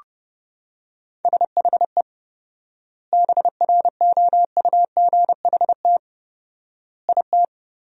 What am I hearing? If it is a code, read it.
Morse code, heard as a keyed tone, about 32 wpm, 700 Hz. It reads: SHE BROUGHT IT
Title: 4x Word-Spacing - Sentences from Top 500 Words - 30wpm